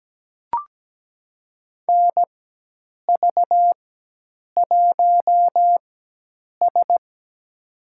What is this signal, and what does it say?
Morse code, heard as a keyed tone, about 17 wpm, spaced as for 8 wpm, 700 Hz. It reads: NV1S
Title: USA Call Signs - 4 Characters - 4x Character Spacing - 17wpm